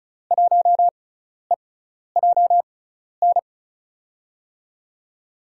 Morse code, heard as a keyed tone, about 36 wpm, spaced as for 12 wpm, 700 Hz. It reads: 1EJN
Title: USA Call Signs - 5 Characters - 6x Character Spacing - 35wpm